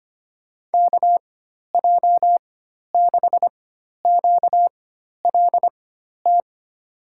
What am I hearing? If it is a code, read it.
Morse code, heard as a keyed tone, about 25 wpm, spaced as for 12 wpm, 700 Hz. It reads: KJ6QLT
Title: USA Call Signs - 6 Characters - 4x Character Spacing - 25wpm